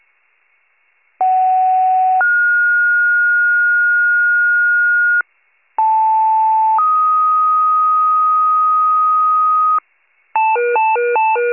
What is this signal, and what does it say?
Two-tone page, heard as a one-second tone, then a three-second tone, about 750 then 1430 Hz, about 880 then 1250 Hz, with noise. It is heard from 1.2 to 9.8 s.